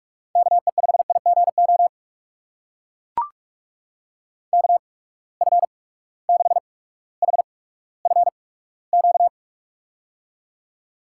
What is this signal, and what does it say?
Morse code, heard as a keyed tone, about 44 wpm, spaced as for 19 wpm, 700 Hz. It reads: KE5IGO KF6HFQ
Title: USA Call Signs - 6 Characters - 8x Character Spacing - 45wpm